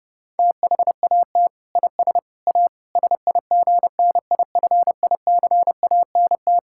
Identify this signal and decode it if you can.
Morse code, heard as a keyed tone, about 30 wpm, 700 Hz. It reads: THAT IS A SIGNIFICANT
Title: Sentences from Top 1000 Words - Review Part 6 - 30wpm